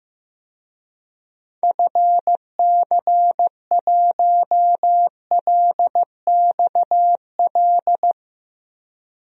Morse code, heard as a keyed tone, about 15 wpm, 700 Hz. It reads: FC1LXL
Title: DXCC-100 Call Signs - 6 Characters - 15wpm